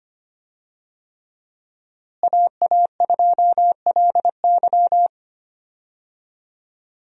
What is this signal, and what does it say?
Morse code, heard as a keyed tone, about 25 wpm, 700 Hz. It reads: AA2LY
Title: USA Call Signs - 5 Characters - Repeated 3x - 25wpm